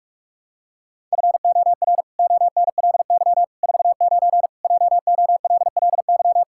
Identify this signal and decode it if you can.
Morse code, heard as a keyed tone, about 41 wpm, 700 Hz. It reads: FOR ONLY 49 JOLLY
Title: Pangram Challenge v2 - 45wpm